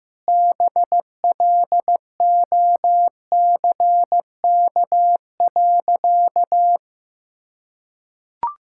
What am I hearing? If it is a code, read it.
Morse code, heard as a keyed tone, about 15 wpm, 700 Hz. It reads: BLOCK.
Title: Sentences from Top 1600 Words 15wpm